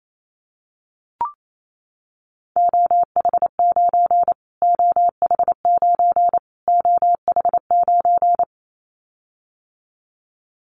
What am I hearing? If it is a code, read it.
Morse code, heard as a keyed tone, about 28 wpm, 700 Hz. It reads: OH9 OH9 OH9
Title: Callsign Prefixes for Top 25 DXCC - Repeated 3x - 28wpm